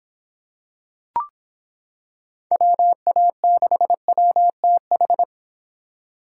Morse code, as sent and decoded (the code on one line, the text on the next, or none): .-- .- -.... .-- - ....
WA6WTH